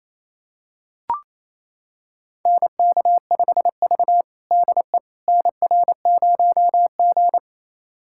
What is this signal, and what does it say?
Morse code, heard as a keyed tone, about 28 wpm, 700 Hz. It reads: NK5V DE NR0G
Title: USA Callsign DE Callsign - 4 Characters - 28wpm